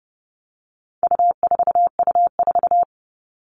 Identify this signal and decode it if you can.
Morse code, heard as a keyed tone, about 30 wpm, 700 Hz. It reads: U4U4
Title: All Cut Numbers - 4-Digit 30wpm